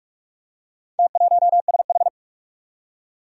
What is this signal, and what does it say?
Morse code, heard as a keyed tone, about 44 wpm, 700 Hz. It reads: T1SH
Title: Four Challenging Letter-Number Combinations 45wpm